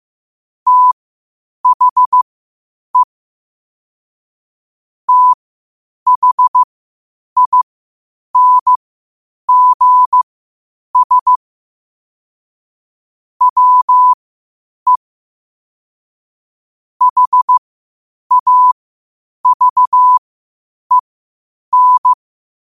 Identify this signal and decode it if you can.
Morse code, heard as a keyed tone, about 15 wpm, spaced as for 8 wpm, 1000 Hz. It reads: THE THINGS WE HAVEN
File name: qotdfiles / 8wpm